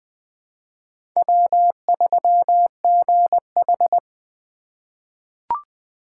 Morse code, heard as a keyed tone, about 21 wpm, 700 Hz. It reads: W3GH
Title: USA Call Signs - 4 Characters - 4x Character Spacing - 20wpm